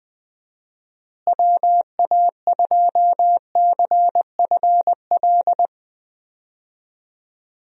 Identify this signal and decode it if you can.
Morse code, heard as a keyed tone, about 21 wpm, 700 Hz. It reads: WA2CFL
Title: USA Call Signs - 6 Characters - Repeated 3x - 20wpm